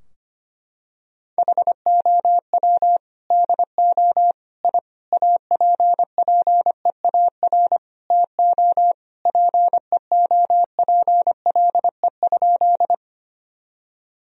Morse code, heard as a keyed tone, about 25 wpm, 700 Hz. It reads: HOW DO I APPEAR TO PEOPLE?
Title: Sentences from Top 1000 Words - Review Part 2 - 25wpm